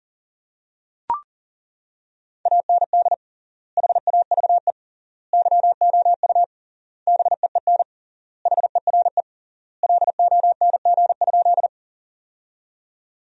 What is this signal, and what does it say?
Morse code, heard as a keyed tone, about 40 wpm, 700 Hz. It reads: AND HAVE YOU BEEN HERE LONG?
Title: 3x Word-Spacing - Sentences from Top 200 Words - 40wpm